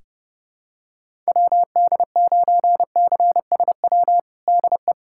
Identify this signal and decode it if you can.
Morse code, heard as a keyed tone, about 32 wpm, 700 Hz. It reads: WD9CSW DE
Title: USA Callsign DE Callsign 30wpm